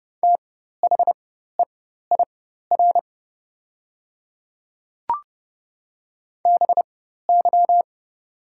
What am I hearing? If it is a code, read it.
Morse code, heard as a keyed tone, about 30 wpm, spaced as for 14 wpm, 700 Hz. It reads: THEIR BY
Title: 4x Character Spacing - Top 200 Words - 30wpm